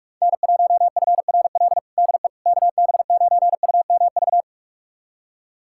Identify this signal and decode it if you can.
Morse code, heard as a keyed tone, about 46 wpm, 700 Hz. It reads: N1FRL DE KB9UMV
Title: USA Callsign DE Callsign 45wpm